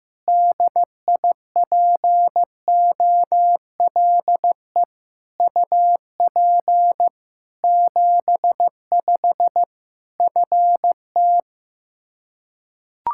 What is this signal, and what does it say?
Morse code, heard as a keyed tone, about 15 wpm, 700 Hz. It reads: DIPOLE UP 75 FT
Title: Practice QSO - Exchange 2 - Info - 15wpm